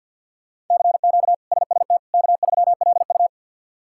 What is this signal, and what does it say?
Morse code, heard as a keyed tone, about 53 wpm, 700 Hz. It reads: CQ SST K3LU
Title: K1USN SST QSO Practice - 50wpm